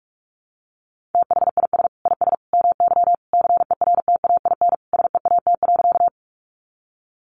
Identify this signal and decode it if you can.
Morse code, heard as a keyed tone, about 44 wpm, 700 Hz. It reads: THIS IS MY CERTAIN SEAT.
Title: Sentences from Top 1500 Words 45wpm